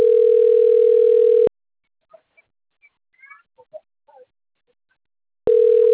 Ringback tone (North America), 440 + 480 Hz.